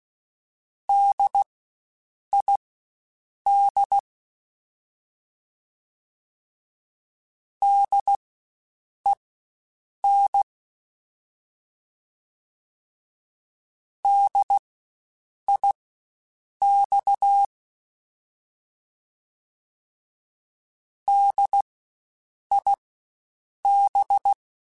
Morse code, heard as a keyed tone, about 16 wpm, spaced as for 7 wpm, 800 Hz. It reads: DID DEN DIX DIB
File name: Spaced Alphas / Spaced Alphas mp3s / Ds sp @15